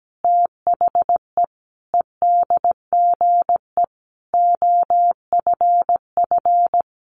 Morse code, heard as a keyed tone, about 17 wpm, 700 Hz. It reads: THE EDGE OFF
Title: Sentences from Top 1300 Words 17wpm